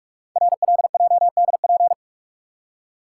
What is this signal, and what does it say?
Morse code, heard as a keyed tone, about 44 wpm, 700 Hz. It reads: RLJDP
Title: Code Groups - 5 Letters - 45wpm